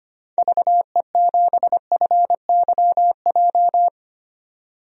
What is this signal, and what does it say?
Morse code, heard as a keyed tone, about 25 wpm, 700 Hz. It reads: VE7FYJ
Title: DXCC-100 Call Signs - 6 Characters - 25wpm